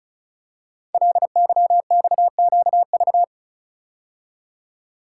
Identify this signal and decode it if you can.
Morse code, heard as a keyed tone, about 36 wpm, 700 Hz. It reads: LYXQV
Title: Code Groups - 5 Letters - 35wpm